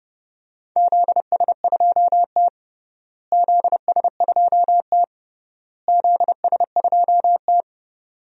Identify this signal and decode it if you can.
Morse code, heard as a keyed tone, about 32 wpm, 700 Hz. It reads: ZS2T ZS2T ZS2T
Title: DXCC-100 Call Signs - 4 Characters - Repeated 3x - 30wpm